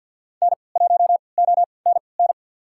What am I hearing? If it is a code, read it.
Morse code, heard as a keyed tone, about 53 wpm, spaced as for 29 wpm, 700 Hz. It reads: N1ONN
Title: USA Call Signs - 5 Characters - 3x Character Spacing - 50wpm